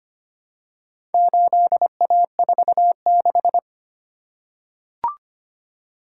Morse code, heard as a keyed tone, about 25 wpm, 700 Hz. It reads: 8A46
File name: practice / 25wpm / Four Letter-Number Combinations - 25wpm